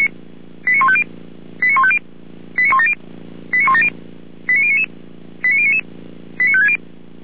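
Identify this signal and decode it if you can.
Five-tone selective call (ZVEI), heard as quick runs of five tones, tones of about 70 ms. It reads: 89150 89150 89170 89170 89099 89099 89560
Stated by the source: ZVEI five-tone call received by SDR, logged on the feuerwehr-orlamuende.org server.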